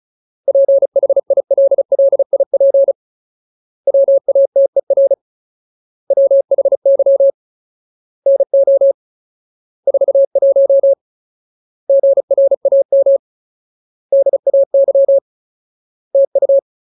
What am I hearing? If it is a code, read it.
Morse code, heard as a keyed tone, about 34 wpm, 550 Hz. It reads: PHILLIP WATER WHY NO 41 GRAM DAY TU
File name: Ultimate Word List / 35WPM Ultimate